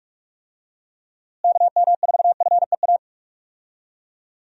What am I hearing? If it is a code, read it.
Morse code, heard as a keyed tone, about 46 wpm, 700 Hz. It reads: KM4FEA
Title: USA Call Signs - 6 Characters - Repeated 3x - 45wpm